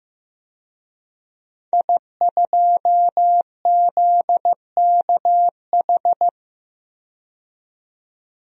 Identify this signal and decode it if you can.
Morse code, heard as a keyed tone, about 15 wpm, 700 Hz. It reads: I2ZKH
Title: DXCC-100 Call Signs - 5 Characters - 15wpm